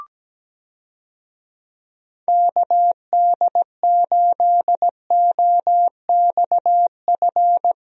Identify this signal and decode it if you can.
Morse code, heard as a keyed tone, about 17 wpm, 700 Hz. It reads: KD8OXF
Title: USA Call Signs - 6 Characters - Repeated 3x - 17wpm